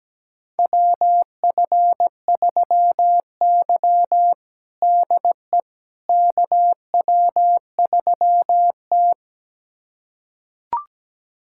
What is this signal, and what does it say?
Morse code, heard as a keyed tone, about 17 wpm, 700 Hz. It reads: WF3Y DE KW3T
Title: USA Callsign DE Callsign - 4 Characters - 17wpm